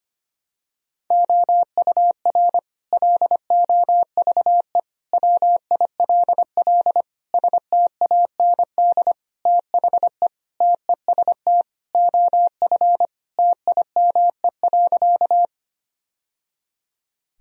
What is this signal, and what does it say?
Morse code, heard as a keyed tone, about 25 wpm, 700 Hz. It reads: OUR LOVE WILL STAND THE TEST OF TIME.
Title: Sentences from Top 1000 Words - Review Part 3 - 25wpm